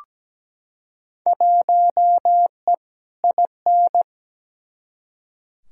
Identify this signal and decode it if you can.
Morse code, heard as a keyed tone, about 17 wpm, 700 Hz. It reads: 1E IN
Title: Field Day Exchanges - 17wpm